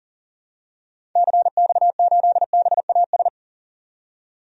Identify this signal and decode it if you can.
Morse code, heard as a keyed tone, about 38 wpm, 700 Hz. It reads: CX8BAS
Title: DXCC-100 Call Signs - 6 Characters - 40wpm